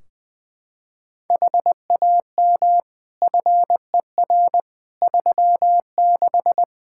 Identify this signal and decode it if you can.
Morse code, heard as a keyed tone, about 20 wpm, 700 Hz. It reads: HAM FER 36
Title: Practice QSO - Exchange 3 - More Info - 20wpm